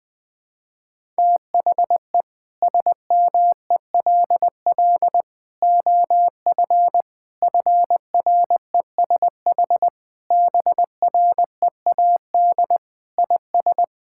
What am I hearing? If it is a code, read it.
Morse code, heard as a keyed tone, about 20 wpm, 700 Hz. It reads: THE SMELL OF FRESH BREAD IS
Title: Difficult Sentences with Hints - 20wpm